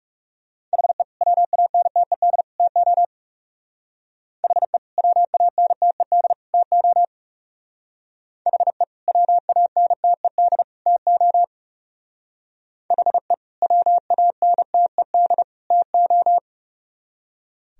Morse code, heard as a keyed tone, about 38 wpm, 700 Hz. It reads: HE WANTED TO HE WANTED TO HE WANTED TO HE WANTED TO
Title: Speed-Racing - Sets of 3 Words 30wpm